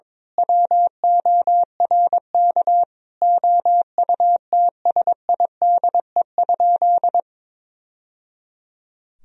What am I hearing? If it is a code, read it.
Morse code, heard as a keyed tone, about 22 wpm, 700 Hz. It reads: WORK OUTSIDE?
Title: Sentences from Top 1000 Words - Review Part 4 - 22wpm